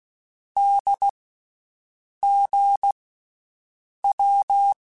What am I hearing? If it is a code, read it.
Morse code, heard as a keyed tone, about 16 wpm, 800 Hz. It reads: D G W
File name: Basic Sets / Basic Letter Groups Collective7 / AWUNDG@13